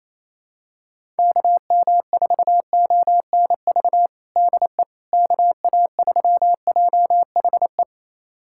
Morse code, heard as a keyed tone, about 27 wpm, 700 Hz. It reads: KM4ONV DE KA3JHE
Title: USA Callsign DE Callsign - 6 Characters - 28wpm